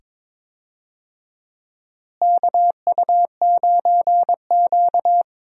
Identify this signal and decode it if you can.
Morse code, heard as a keyed tone, about 22 wpm, 700 Hz. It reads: KU9Q
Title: USA Call Signs - 4 Characters - Repeated 2x - 22wpm